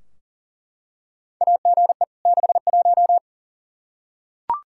Morse code, heard as a keyed tone, about 40 wpm, 700 Hz. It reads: AGE 61
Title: Practice QSO - Exchange 3 - More Info - 40wpm